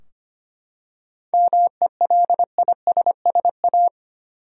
Morse code, heard as a keyed tone, about 25 wpm, 700 Hz. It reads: MELISSA
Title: USA Common Names - 25wpm